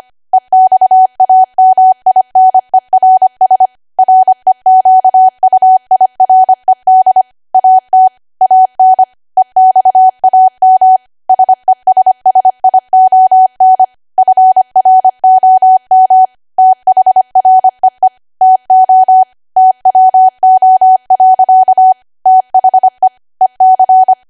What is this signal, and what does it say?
Morse code, heard as a keyed tone, about 25 wpm, 750 Hz. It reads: EXAMINERS REQUIRED AT AN EXAM SESSION FROM THREE TO TWO. THE EC